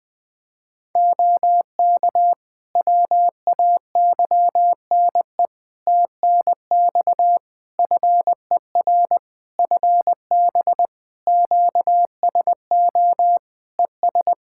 Morse code, heard as a keyed tone, about 20 wpm, 700 Hz. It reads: OK WAYNE TNX FER FB QSO ES
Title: Practice QSO - Exchange 4 - Ending - 20wpm